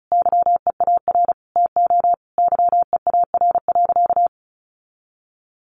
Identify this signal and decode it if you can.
Morse code, heard as a keyed tone, about 34 wpm, 700 Hz. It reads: YEAR TO YEAR.
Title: Sentences from Top 1200 Words 35wpm